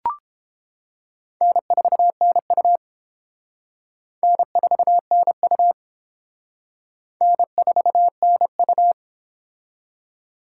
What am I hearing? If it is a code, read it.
Morse code, heard as a keyed tone, about 29 wpm, 700 Hz. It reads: N4NU N4NU N4NU